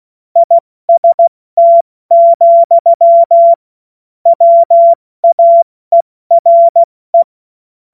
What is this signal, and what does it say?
Morse code, heard as a keyed tone, about 16 wpm, 680 Hz. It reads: IST, WAERE
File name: Morsekurs / Witze / Witz 05 März 2024 70 F80